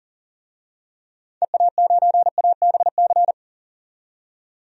Morse code, heard as a keyed tone, about 40 wpm, 700 Hz. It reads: EA9ABC
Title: DXCC-100 Call Signs - 6 Characters - 40wpm